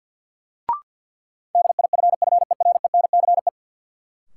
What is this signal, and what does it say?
Morse code, heard as a keyed tone, about 53 wpm, 700 Hz. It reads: DIFFERENCE
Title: Sentences from Top 800 Words - New Word Drill - 50wpm